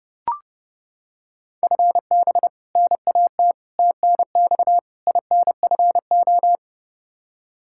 Morse code, heard as a keyed tone, about 30 wpm, 700 Hz. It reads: FB NAT TNX INFO
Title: Practice QSO - Exchange 3 - More Info - 30wpm